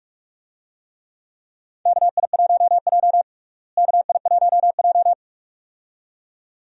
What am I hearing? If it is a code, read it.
Morse code, heard as a keyed tone, about 46 wpm, 700 Hz. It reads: KI1J KI1J
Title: USA Call Signs - 4 Characters - Repeated 2x - 45wpm